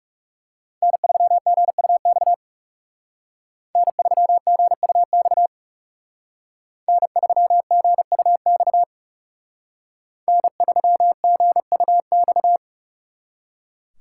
Morse code, heard as a keyed tone, about 38 wpm, 700 Hz. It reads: N3GUX N3GUX N3GUX N3GUX